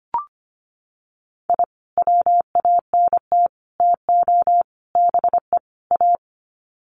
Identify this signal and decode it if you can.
Morse code, heard as a keyed tone, about 25 wpm, 700 Hz. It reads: I WANT TO BE A
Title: Sets of 5 Words Encore 25wpm